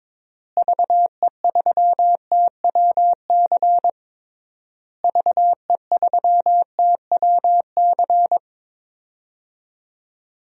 Morse code, heard as a keyed tone, about 22 wpm, 700 Hz. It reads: VE3TWC VE3TWC
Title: DXCC-100 Call Signs - 6 Characters - Repeated 3x - 22wpm